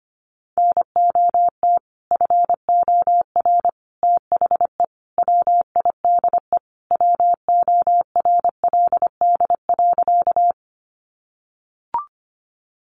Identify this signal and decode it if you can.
Morse code, heard as a keyed tone, about 25 wpm, 700 Hz. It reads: NOT FOR THE WIDE WORLD.
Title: Sentences from Top 900 Words 25wpm